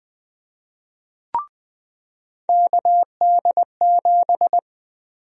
Morse code, heard as a keyed tone, about 20 wpm, 700 Hz. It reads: KD7
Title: USA Call Sign Prefixes - 20wpm